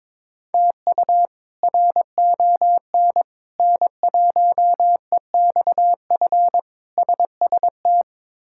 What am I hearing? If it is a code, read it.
Morse code, heard as a keyed tone, about 22 wpm, 700 Hz. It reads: TU RON N1EXF SST
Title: K1USN SST QSO Practice - 22wpm